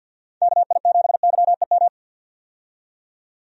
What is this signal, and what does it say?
Morse code, heard as a keyed tone, about 50 wpm, 700 Hz. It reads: KI7YEM
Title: USA Call Signs - 6 Characters - 50wpm